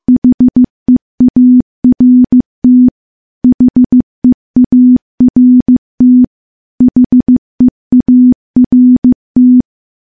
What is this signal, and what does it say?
Morse code, heard as a keyed tone, about 15 wpm, 250 Hz. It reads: HEART HEART HEART
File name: morse-training-15WPM